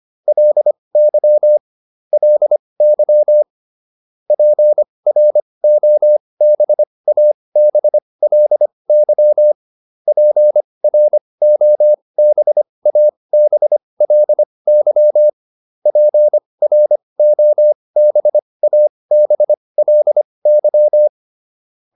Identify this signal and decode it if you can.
Morse code, heard as a keyed tone, about 25 wpm, spaced as for 20 wpm, 600 Hz. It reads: LY LY PROBABLY PROBABLY PROBABLY